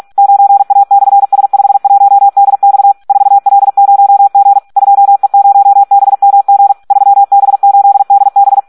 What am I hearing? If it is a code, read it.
Morse code, heard as a keyed tone, about 46 wpm, 800 Hz. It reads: 9A/S51DX 4L0G 2E0BMG 3B8DB
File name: Audio practica CW / SD 1 46